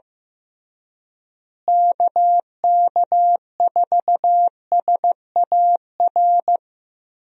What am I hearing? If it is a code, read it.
Morse code, heard as a keyed tone, about 15 wpm, 700 Hz. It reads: KK4SAR
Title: USA Call Signs - 6 Characters - Repeated 3x - 15wpm